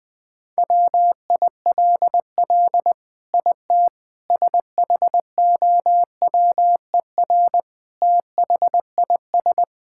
Morse code, heard as a keyed tone, about 20 wpm, 700 Hz. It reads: WILL IT SHOWER THIS
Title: Sentences from Top 1600 Words 20wpm